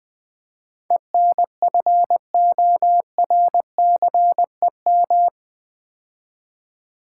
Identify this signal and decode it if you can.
Morse code, heard as a keyed tone, about 20 wpm, 700 Hz. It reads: ENFORCEM
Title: Build A Word - 4 Letters At A Time - 11-Letter Words - 20wpm